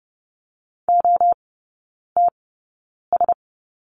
Morse code, heard as a keyed tone, about 30 wpm, spaced as for 9 wpm, 700 Hz. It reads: OTS
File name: 081 Lesson 14e - Number 2 - Three Characters 30wpm